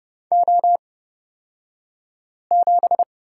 Morse code, heard as a keyed tone, about 30 wpm, spaced as for 5 wpm, 700 Hz. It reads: O7
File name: Lesson 37d - Letter x - Two Characters 30wpm